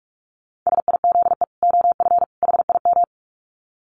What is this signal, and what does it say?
Morse code, heard as a keyed tone, about 44 wpm, 700 Hz. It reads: SIZE OF HIM
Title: Dit Challenge 45wpm